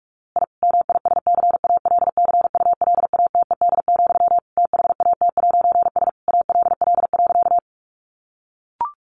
Morse code, heard as a keyed tone, about 44 wpm, 700 Hz. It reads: I MISCALCULATED, THAT'S ALL.
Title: A Sound Of Thunder 45wpm